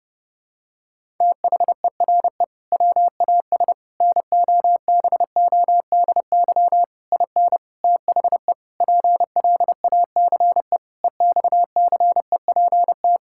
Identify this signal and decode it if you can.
Morse code, heard as a keyed tone, about 32 wpm, 700 Hz. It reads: THERE WAS NOBODY IN THE PLACE EXCEPT
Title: Sentences from Top 1600 Words 30wpm